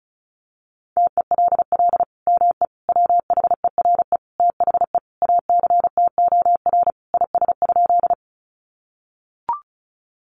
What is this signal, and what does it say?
Morse code, heard as a keyed tone, about 34 wpm, 700 Hz. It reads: TELL ME WHERE THE ACTOR IS?
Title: Sentences from Top 1500 Words 35wpm